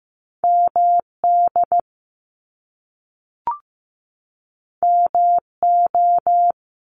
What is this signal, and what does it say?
Morse code, heard as a keyed tone, about 15 wpm, 700 Hz. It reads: MD MO
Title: US States 15wpm